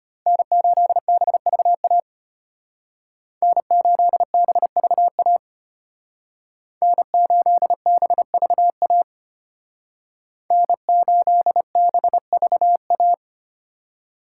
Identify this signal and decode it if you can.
Morse code, heard as a keyed tone, about 32 wpm, 700 Hz. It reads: N8BVA N8BVA N8BVA N8BVA